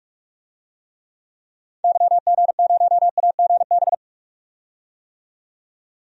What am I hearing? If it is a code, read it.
Morse code, heard as a keyed tone, about 42 wpm, 700 Hz. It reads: YG0AGB
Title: DXCC-100 Call Signs - 6 Characters - 45wpm